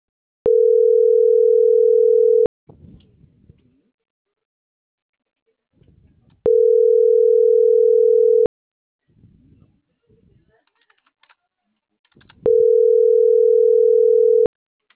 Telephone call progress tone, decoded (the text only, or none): ringback tone